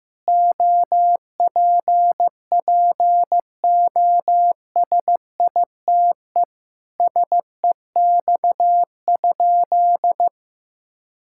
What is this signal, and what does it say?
Morse code, heard as a keyed tone, about 15 wpm, 700 Hz. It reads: OPPOSITE SEX?
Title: Sentences from Top 1600 Words 15wpm